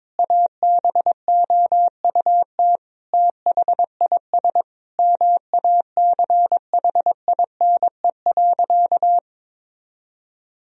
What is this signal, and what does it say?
Morse code, heard as a keyed tone, about 22 wpm, 700 Hz. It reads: ABOUT THIS MACHINE.
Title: Sentences from Top 1000 Words - Review Part 2 - 22wpm